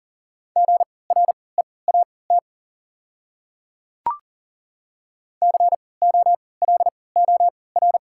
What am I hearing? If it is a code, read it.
Morse code, heard as a keyed tone, about 38 wpm, spaced as for 22 wpm, 700 Hz. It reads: GREAT COLOR
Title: 3x Character Spacing - Top 500 Words - 40wpm